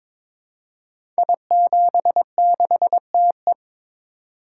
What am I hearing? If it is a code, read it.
Morse code, heard as a keyed tone, about 22 wpm, 700 Hz. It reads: I76TE
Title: Code Groups - 5 Letter:Numbers - 22wpm